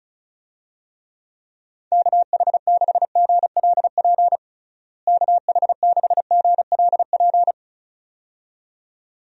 Morse code, heard as a keyed tone, about 36 wpm, 700 Hz. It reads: KH6GLP KH6GLP
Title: DXCC-100 Call Signs - 6 Characters - Repeated 2x - 35wpm